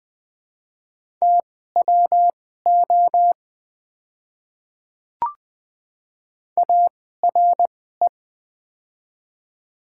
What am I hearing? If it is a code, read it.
Morse code, heard as a keyed tone, about 20 wpm, spaced as for 15 wpm, 700 Hz. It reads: TWO ARE
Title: 2x Character Spacing - Top 200 Words - 20wpm